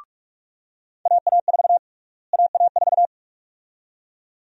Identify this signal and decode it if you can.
Morse code, heard as a keyed tone, about 46 wpm, 700 Hz. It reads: AA4 AA4
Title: USA Call Sign Prefixes - Repeated 2x - 45wpm